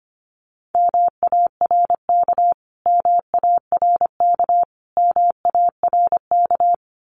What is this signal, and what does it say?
Morse code, heard as a keyed tone, about 25 wpm, 700 Hz. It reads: MARK MARK MARK
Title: Top 500 Words Repeated 25wpm